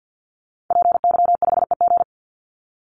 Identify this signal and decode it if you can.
Morse code, heard as a keyed tone, about 50 wpm, 700 Hz. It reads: PY5EG
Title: Call Signs - 50wpm